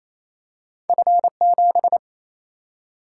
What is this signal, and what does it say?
Morse code, heard as a keyed tone, about 28 wpm, 700 Hz. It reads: F7
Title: Single-Letter Single-Number 28wpm